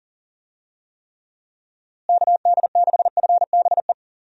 Morse code, heard as a keyed tone, about 40 wpm, 700 Hz. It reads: KD6FBE
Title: USA Call Signs - 6 Characters - Repeated 2x - 40wpm